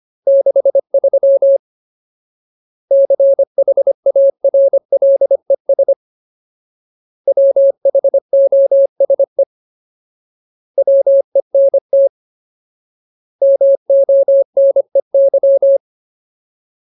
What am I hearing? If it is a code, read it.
Morse code, heard as a keyed tone, about 25 wpm, 550 Hz. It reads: 63 CHARLES WHOSE WENT MONEY
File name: Ultimate Word List / 25WPM Ultimate